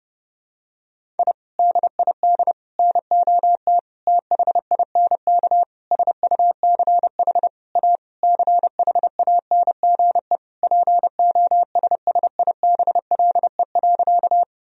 Morse code, heard as a keyed tone, about 32 wpm, 700 Hz. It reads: I DID NOT THINK SUCH A CHANGE POSSIBLE.
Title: Sentences from Top 1000 Words - Review Part 1 - 30wpm